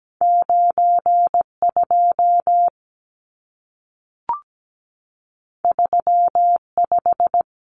Morse code, heard as a keyed tone, about 17 wpm, 700 Hz. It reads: 92 35
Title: Numbers - 2-Digit 17wpm